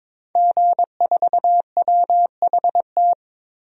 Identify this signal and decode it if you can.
Morse code, heard as a keyed tone, about 22 wpm, 700 Hz. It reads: G4WHT
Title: DXCC-100 Call Signs - 5 Characters - 22wpm